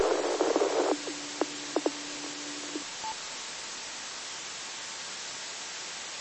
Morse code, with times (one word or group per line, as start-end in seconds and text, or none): none